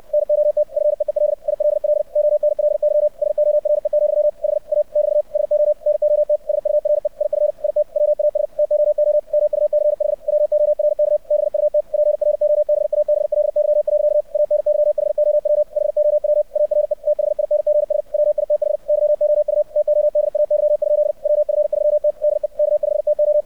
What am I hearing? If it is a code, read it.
Morse code, heard as a keyed tone, about 64 wpm, 600 Hz. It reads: NOT FEEL ILL ONLY SORE, HA <BT> SO NOT SURE IF IT WAS TOO MUCH WORK BUT PROBABLY. ANYHOW HOW ARE THINGS WITH YOU TODAY? WF3T DE K5TO
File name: archived / CRSnet